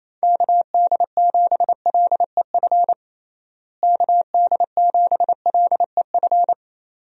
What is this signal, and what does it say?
Morse code, heard as a keyed tone, about 27 wpm, 700 Hz. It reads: KD7LEF KD7LEF
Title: USA Call Signs - 6 Characters - Repeated 2x - 28wpm